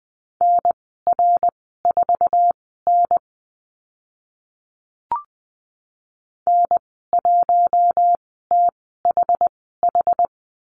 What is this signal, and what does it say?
Morse code, heard as a keyed tone, about 20 wpm, spaced as for 14 wpm, 700 Hz. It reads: NR4N N1THH
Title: 069 Lesson 12g - Letter d - Call Signs 20wpm